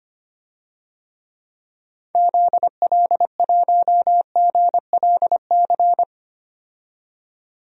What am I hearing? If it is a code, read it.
Morse code, heard as a keyed tone, about 25 wpm, 700 Hz. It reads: ZL1GLC
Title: DXCC-100 Call Signs - 6 Characters - 25wpm